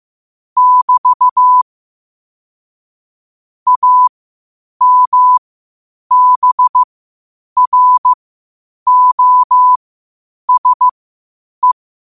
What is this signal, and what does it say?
Morse code, heard as a keyed tone, about 15 wpm, spaced as for 8 wpm, 1000 Hz. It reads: <BT> AMBROSE
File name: qotdfiles / 8wpm